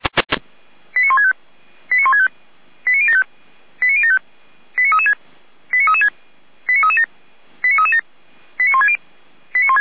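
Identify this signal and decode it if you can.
Five-tone selective call (ZVEI), heard as quick runs of five tones, tones of about 70 ms. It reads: 89165 89165 89075 89075 89336 89336 89337 89337 89160
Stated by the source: ZVEI five-tone call 89073 received on SDR, recorded by feuerwehr-orlamuende.org fire brigade.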